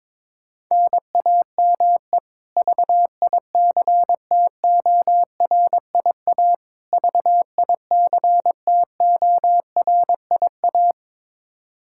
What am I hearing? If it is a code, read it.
Morse code, heard as a keyed tone, about 22 wpm, 700 Hz. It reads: NAME VICTORIA VICTORIA
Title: K1USN SST QSO Practice 22wpm